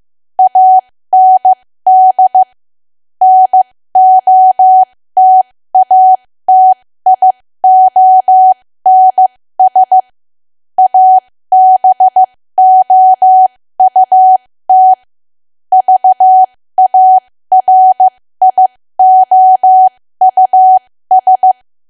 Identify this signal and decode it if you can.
Morse code, heard as a keyed tone, about 15 wpm, spaced as for 13 wpm, 750 Hz. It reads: AND NOTATIONS ABOUT VARIOUS
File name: cwa / W1AW-Level2 / 13WPM